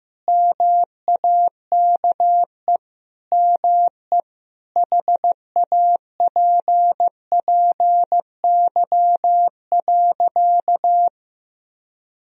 Morse code, heard as a keyed tone, about 15 wpm, 700 Hz. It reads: MAKE ME HAPPY.
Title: Sentences from Top 1000 Words - Review Part 2 - 15wpm